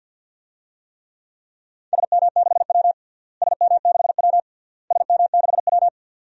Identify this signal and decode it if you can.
Morse code, heard as a keyed tone, about 47 wpm, 700 Hz. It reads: SM6W SM6W SM6W
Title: DXCC-100 Call Signs - 4 Characters - Repeated 3x - 50wpm